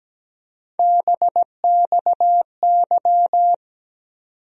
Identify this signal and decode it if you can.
Morse code, heard as a keyed tone, about 17 wpm, 700 Hz. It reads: BXY
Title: USA Call Sign Postfixes - 3 Letters 17wpm